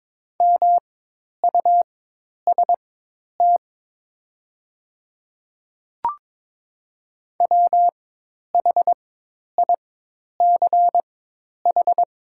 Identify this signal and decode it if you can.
Morse code, heard as a keyed tone, about 22 wpm, spaced as for 10 wpm, 700 Hz. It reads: MUST WHICH
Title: 4x Character Spacing - Top 200 Words - 22wpm